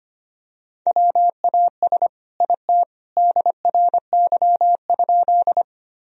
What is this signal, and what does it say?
Morse code, heard as a keyed tone, about 25 wpm, 700 Hz. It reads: WAS IT DRY?
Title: Sentences from Top 1000 Words - Review Part 3 - 25wpm